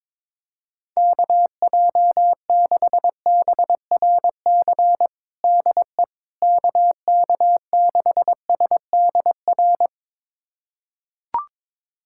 Morse code, heard as a keyed tone, about 22 wpm, 700 Hz. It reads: KJ6BRC DE KK6SDR
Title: USA Callsign DE Callsign - 6 Characters - 22wpm